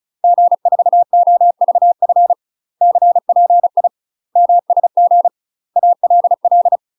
Morse code, heard as a keyed tone, about 36 wpm, 700 Hz. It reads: G4OVF CPI MSG ALL
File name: UK MASTER QSOs TRAINING / 35WPM / UK MASTER QSOs 0033